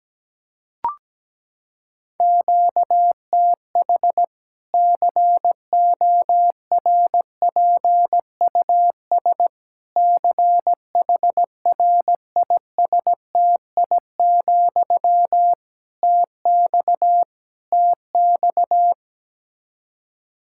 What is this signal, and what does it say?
Morse code, heard as a keyed tone, about 17 wpm, 700 Hz. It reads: QTH CORPUS CHRISTI, TX TX
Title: Practice QSO - Exchange 1 - Beginning - 17wpm